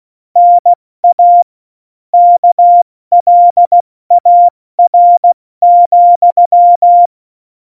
Morse code, heard as a keyed tone, about 16 wpm, 700 Hz. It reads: NA KLAR,
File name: Morsekurs / Witze / Witz 19. Januar 2021, 70 F80